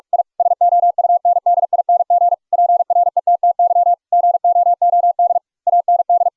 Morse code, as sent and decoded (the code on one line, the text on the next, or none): .. / ... --- ..- -. -.. .. -. --. / .--. .-. - - -.-- / --. --- --- -.. / .- -. -..
I SOUNDING PRETTY GOOD AND